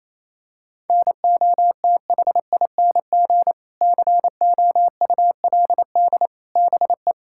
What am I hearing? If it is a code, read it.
Morse code, heard as a keyed tone, about 28 wpm, 700 Hz. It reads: NOTHING COULD BE
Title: Sentences from Top 1600 Words 28wpm